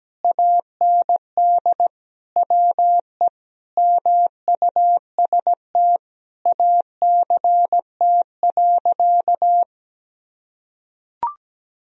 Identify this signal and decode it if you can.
Morse code, heard as a keyed tone, about 17 wpm, 700 Hz. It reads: AND WE MUST ACT.
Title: Sentences from Top 1000 Words - Review Part 1 - 17wpm